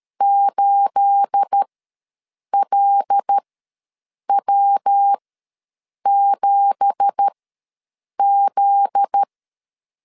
Morse code, heard as a keyed tone, about 12 wpm, spaced as for 7 wpm, 800 Hz. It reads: 8LW7Z
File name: LEZIONI / 9 2867 ws 12wpm 3431